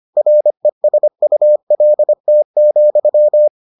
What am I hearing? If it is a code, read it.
Morse code, heard as a keyed tone, about 25 wpm, 600 Hz. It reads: RESULT,